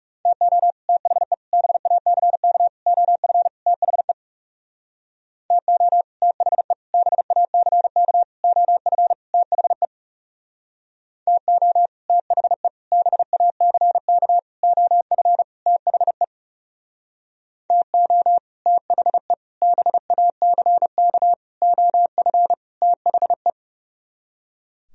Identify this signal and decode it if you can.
Morse code, heard as a keyed tone, about 38 wpm, 700 Hz. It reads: TO THE BACK OF THE TO THE BACK OF THE TO THE BACK OF THE TO THE BACK OF THE